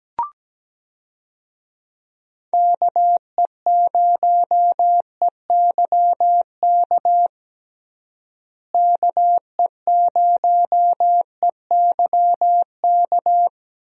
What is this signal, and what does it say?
Morse code, heard as a keyed tone, about 17 wpm, 700 Hz. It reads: KE0EYK KE0EYK
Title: USA Call Signs - 6 Characters - Repeated 3x - 17wpm